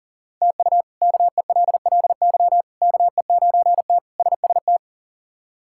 Morse code, heard as a keyed tone, about 40 wpm, 700 Hz. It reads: TU KELLY KE9T SST
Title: K1USN SST QSO Practice - 40wpm